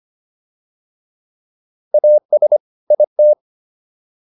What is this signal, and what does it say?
Morse code, heard as a keyed tone, about 25 wpm, 600 Hz. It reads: AS IT